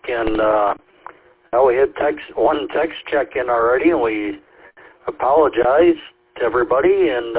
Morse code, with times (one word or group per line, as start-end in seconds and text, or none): none